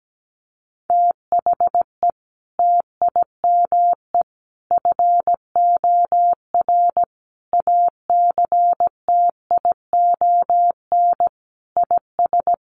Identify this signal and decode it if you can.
Morse code, heard as a keyed tone, about 17 wpm, 700 Hz. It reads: THE TIME FOR ACTION IS
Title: Sentences from Top 700 Words 17wpm